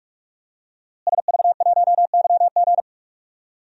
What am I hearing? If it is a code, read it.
Morse code, heard as a keyed tone, about 46 wpm, 700 Hz. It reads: SV1YG
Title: DXCC-100 Call Signs - 5 Characters - Repeated 2x - 45wpm